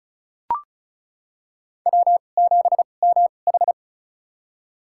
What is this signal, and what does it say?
Morse code, heard as a keyed tone, about 34 wpm, spaced as for 25 wpm, 700 Hz. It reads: W7MH
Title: USA Call Signs - 4 Characters - 2x Character Spacing - 35wpm